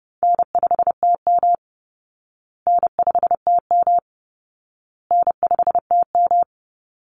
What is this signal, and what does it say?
Morse code, heard as a keyed tone, about 30 wpm, 700 Hz. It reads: N5TM N5TM N5TM
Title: Easy Call Signs Repeated 3x - 30wpm